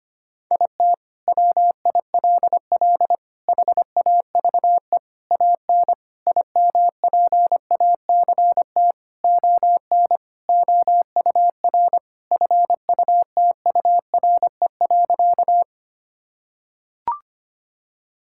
Morse code, heard as a keyed tone, about 25 wpm, 700 Hz. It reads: IT WILL HAVE AN IMPACT ON OUR FUTURE.
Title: Sentences from Top 1000 Words - Review Part 6 - 25wpm